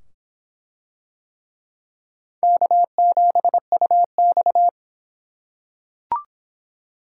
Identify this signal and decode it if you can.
Morse code, heard as a keyed tone, about 25 wpm, 700 Hz. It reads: K7UX